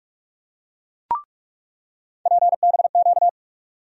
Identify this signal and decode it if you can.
Morse code, heard as a keyed tone, about 42 wpm, 700 Hz. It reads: PBQ
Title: USA Call Sign Postfixes - 3 Letters 45wpm